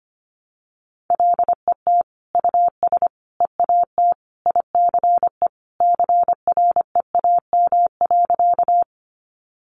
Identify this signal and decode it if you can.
Morse code, heard as a keyed tone, about 25 wpm, 700 Hz. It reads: LET US EAT ICE CREAM.
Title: Sentences from Top 1500 Words 25wpm